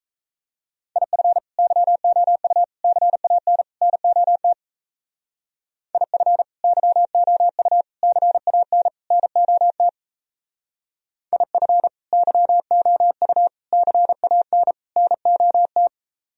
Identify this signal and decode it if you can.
Morse code, heard as a keyed tone, about 34 wpm, 700 Hz. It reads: IF YOU CAN NOT IF YOU CAN NOT IF YOU CAN NOT
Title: Speed-Racing - Sets of 4 Words 28wpm